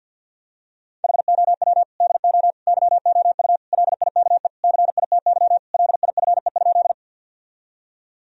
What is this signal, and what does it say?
Morse code, heard as a keyed tone, about 52 wpm, 700 Hz. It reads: HOW DO YOU LIKE CITY LIFE?
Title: Sentences from Top 1000 Words - Review Part 1 - 50wpm